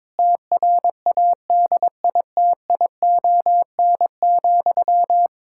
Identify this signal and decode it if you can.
Morse code, heard as a keyed tone, about 22 wpm, 700 Hz. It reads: TRADITION,
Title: Pangram Challenge v2 - 22wpm